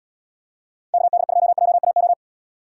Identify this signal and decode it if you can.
Morse code, heard as a keyed tone, about 77 wpm, 700 Hz. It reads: KD2JAO